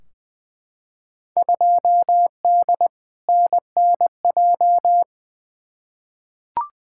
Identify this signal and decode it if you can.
Morse code, heard as a keyed tone, about 20 wpm, 700 Hz. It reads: 2D NNJ
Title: Field Day Exchanges - 20wpm